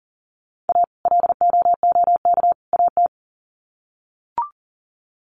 Morse code, heard as a keyed tone, about 40 wpm, 700 Hz. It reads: A LOOK AT
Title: Sets of 3 Words 40wpm